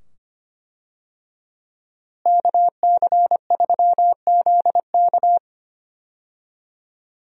Morse code, heard as a keyed tone, about 25 wpm, 700 Hz. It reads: KC3ZK
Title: USA Call Signs - 5 Characters - Repeated 2x - 25wpm